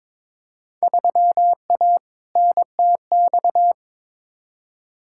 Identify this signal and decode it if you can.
Morse code, heard as a keyed tone, about 22 wpm, 700 Hz. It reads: 3A NTX
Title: Field Day Exchanges - 22wpm